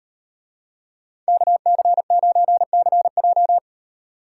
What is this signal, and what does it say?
Morse code, heard as a keyed tone, about 38 wpm, 700 Hz. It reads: KC9CJ